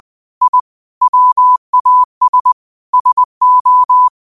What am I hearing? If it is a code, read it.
Morse code, heard as a keyed tone, about 20 wpm, 1000 Hz. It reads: I WAS SO
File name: qotdfiles / 20wpm